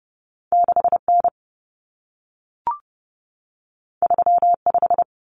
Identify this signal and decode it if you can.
Morse code, heard as a keyed tone, about 30 wpm, 700 Hz. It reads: 6N 35
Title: Common Cut Numbers - 2-Digit 30wpm